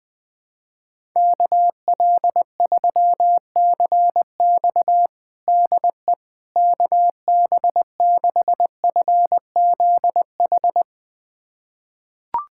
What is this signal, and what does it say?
Morse code, heard as a keyed tone, about 20 wpm, 700 Hz. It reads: KL3CX DE KB6FZH
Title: USA Callsign DE Callsign 20wpm